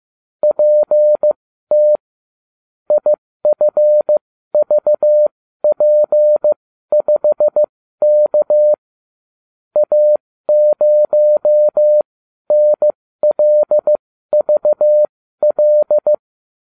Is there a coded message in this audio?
Yes: Morse code from 0.4 to 16.1 s, heard as a keyed tone, about 15 wpm, spaced as for 12 wpm, 600 Hz. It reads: PT IFVP5K A0NLVL